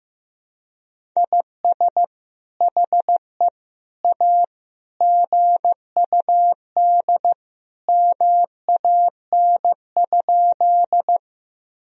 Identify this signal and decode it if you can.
Morse code, heard as a keyed tone, about 15 wpm, 700 Hz. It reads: IS HE A GUD MAN?
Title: Sentences from Top 200 Words and Abbreviations 15wpm